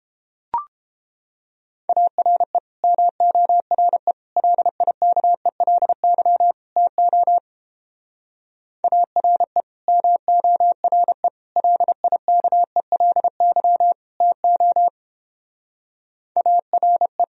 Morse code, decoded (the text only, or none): ARE MORE LIKELY TO ARE MORE LIKELY TO ARE